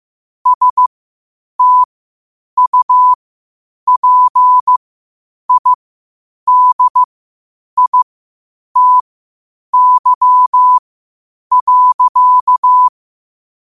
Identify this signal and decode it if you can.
Morse code, heard as a keyed tone, about 15 wpm, spaced as for 8 wpm, 1000 Hz. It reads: STUPIDITY.
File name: qotdfiles / 8wpm